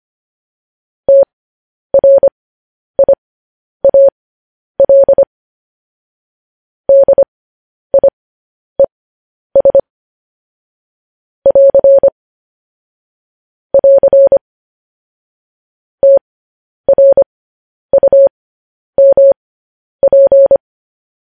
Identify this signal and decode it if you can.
Morse code, heard as a keyed tone, about 25 wpm, spaced as for 10 wpm, 550 Hz. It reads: TRIAL DIES <AR> <AR> TRUMP
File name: podcast / News headlines-1x-25x10